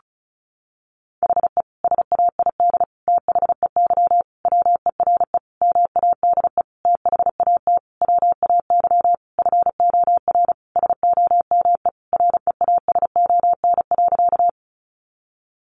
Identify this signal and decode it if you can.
Morse code, heard as a keyed tone, about 34 wpm, 700 Hz. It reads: HE SAID THEY WERE MADE THAT WAY FOR SOME REASON.
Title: Sentences from Top 700 Words 35wpm